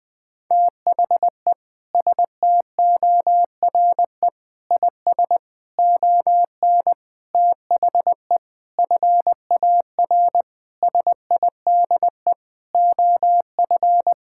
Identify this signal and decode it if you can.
Morse code, heard as a keyed tone, about 20 wpm, 700 Hz. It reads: THE STORE IS ON THE FAR SIDE OF
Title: Sentences with 6-Word Phrases - 20wpm